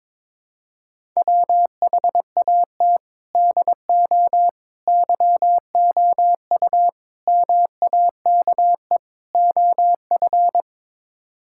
Morse code, heard as a keyed tone, about 22 wpm, 700 Hz. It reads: WHAT DO YOU MAKE OF
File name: practice / 22wpm / Build A Phrase - 5-Word Phrases - 22wpm